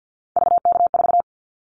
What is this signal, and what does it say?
Morse code, heard as a keyed tone, about 50 wpm, 700 Hz. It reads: VK4
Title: Callsign Prefixes for Top 100 DXCC - 50wpm